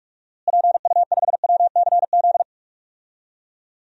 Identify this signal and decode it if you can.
Morse code, heard as a keyed tone, about 41 wpm, 700 Hz. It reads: PU5WCZ
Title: DXCC-100 Call Signs - 6 Characters - Repeated 3x - 45wpm